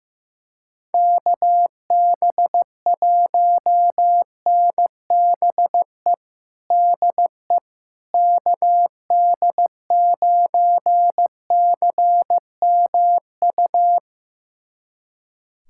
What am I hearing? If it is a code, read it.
Morse code, heard as a keyed tone, about 16 wpm, 700 Hz. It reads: KB1NBE DE KD9CMU
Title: USA Callsign DE Callsign - 6 Characters - 15wpm